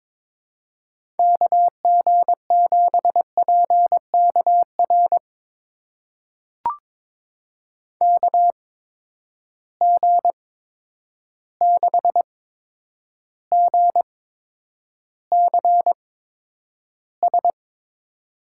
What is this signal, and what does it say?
Morse code, heard as a keyed tone, about 22 wpm, spaced as for 9 wpm, 700 Hz. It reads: KG7PKR KG6GCS
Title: USA Call Signs - 6 Characters - 8x Character Spacing - 22wpm